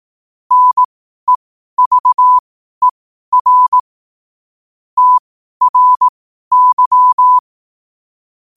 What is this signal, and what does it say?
Morse code, heard as a keyed tone, about 18 wpm, spaced as for 12 wpm, 1000 Hz. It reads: NEVER TRY